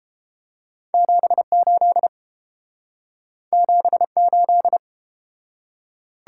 Morse code, heard as a keyed tone, about 31 wpm, 700 Hz. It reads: 78 78